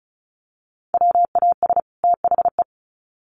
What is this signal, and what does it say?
Morse code, heard as a keyed tone, about 34 wpm, 700 Hz. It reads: WAS THE
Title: Sets of 2 Words 35wpm